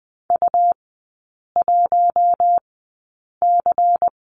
Morse code, heard as a keyed tone, about 20 wpm, spaced as for 8 wpm, 700 Hz. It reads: U1C
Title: 138 Lesson 23e - Letter f - Three Characters 20wpm